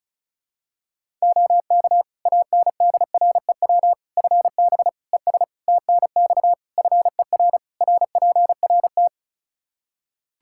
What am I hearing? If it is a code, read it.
Morse code, heard as a keyed tone, about 36 wpm, 700 Hz. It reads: OK ANDREW FB ES TNX FER RPRT
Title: Practice QSO - Exchange 2 - Info - 35wpm